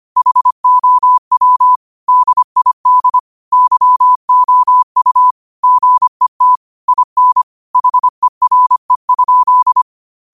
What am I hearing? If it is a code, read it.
Morse code, heard as a keyed tone, about 25 wpm, 1000 Hz. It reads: HOW DID YOU GET IN HERE?
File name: qotdfiles / 25wpm